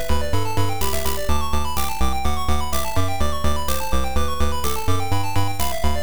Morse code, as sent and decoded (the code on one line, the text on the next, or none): none